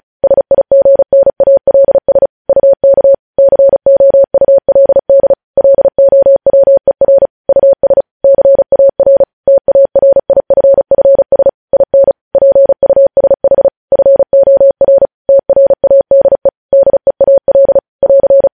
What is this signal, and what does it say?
Morse code, heard as a keyed tone, about 36 wpm, 550 Hz. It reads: SIGNALS UK COULD LOWER US CAR TARIFFS IN PUSH FOR TRADE DEAL <AR>
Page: news headlines in morse code at 35 wpm